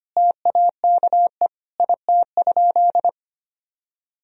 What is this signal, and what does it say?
Morse code, heard as a keyed tone, about 25 wpm, 700 Hz. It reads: TAKE IT?
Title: Sentences from Top 300 Words and Abbreviations 25wpm